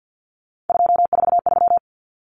Morse code, heard as a keyed tone, about 48 wpm, 700 Hz. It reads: 243
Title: Common Cut Numbers - 3-Digit 50wpm